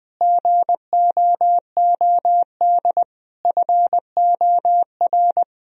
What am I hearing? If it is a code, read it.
Morse code, heard as a keyed tone, about 20 wpm, 700 Hz. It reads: GOOD FOR
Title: Sentences from Top 1000 Words - Review Part 4 - 20wpm